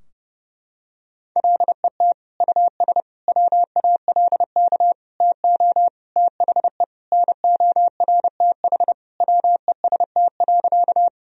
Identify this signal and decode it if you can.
Morse code, heard as a keyed tone, about 30 wpm, 700 Hz. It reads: LET US WALK TO THE NORTH WEST.
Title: Sentences from Top 1000 Words - Review Part 2 - 30wpm